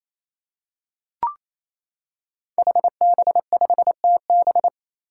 Morse code, heard as a keyed tone, about 27 wpm, 700 Hz. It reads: HB5TB
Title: Five Challenging Letter-Number Combinations 28wpm